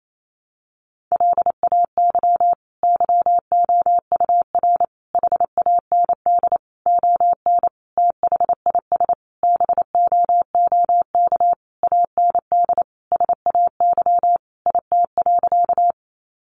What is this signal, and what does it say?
Morse code, heard as a keyed tone, about 28 wpm, 700 Hz. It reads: LAY YOUR HAND ON THIS BOOK AND SAY IT.
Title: Sentences from Top 400 Words 28wpm